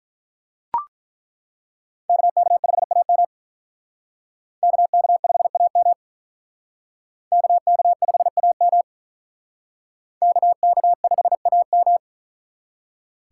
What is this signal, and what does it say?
Morse code, heard as a keyed tone, about 42 wpm, 700 Hz. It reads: KK5AM KK5AM KK5AM KK5AM
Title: Speed Racing - USA Call Signs 35wpm